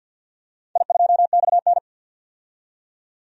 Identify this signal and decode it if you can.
Morse code, heard as a keyed tone, about 51 wpm, 700 Hz. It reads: I2XN
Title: DXCC-100 Call Signs - 4 Characters - 50wpm